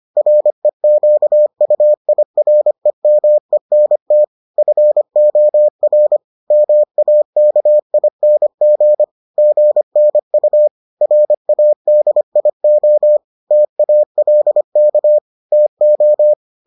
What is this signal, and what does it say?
Morse code, heard as a keyed tone, about 25 wpm, 600 Hz. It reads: REQUIREMENT FOR MAKING GNU RADIO TALK TO